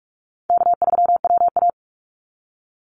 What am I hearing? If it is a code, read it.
Morse code, heard as a keyed tone, about 44 wpm, 700 Hz. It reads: K3WA
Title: Easy Call Signs Repeated 3x - 45wpm